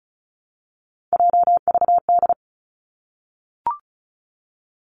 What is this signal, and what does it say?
Morse code, heard as a keyed tone, about 36 wpm, 700 Hz. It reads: JVD
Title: Three Challenging Letter-Number Combinations 35wpm